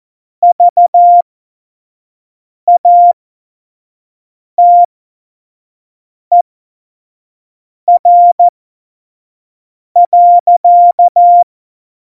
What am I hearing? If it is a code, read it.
Morse code, heard as a keyed tone, about 14 wpm, spaced as for 5 wpm, 700 Hz. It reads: VATER.
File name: Morsekurs / Witze / Witz 06. Oktober 2020, 25 F70